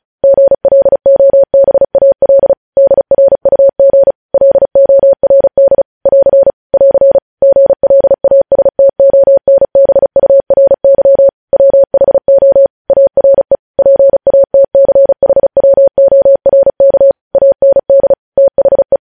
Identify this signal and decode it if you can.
Morse code, heard as a keyed tone, about 36 wpm, 550 Hz. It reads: GLOBAL DRUG LORD <AR> <AR> GLASTONBURY WHO ARE PATCHWORK AND THE
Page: news headlines in morse code at 35 wpm